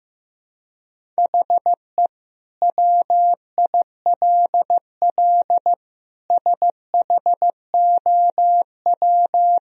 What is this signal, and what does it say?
Morse code, heard as a keyed tone, about 15 wpm, 700 Hz. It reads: HE WILL SHOW
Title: Sentences from Top 1000 Words - Review Part 1 - 15wpm